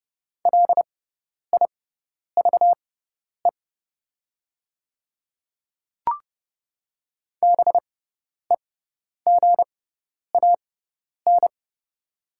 Morse code, heard as a keyed tone, about 30 wpm, spaced as for 10 wpm, 700 Hz. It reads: LIVE BEGAN
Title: 6x Character Spacing - Top 500 Words - 30wpm